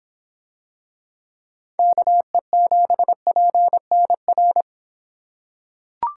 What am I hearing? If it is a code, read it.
Morse code, heard as a keyed tone, about 24 wpm, 700 Hz. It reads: KE7PNR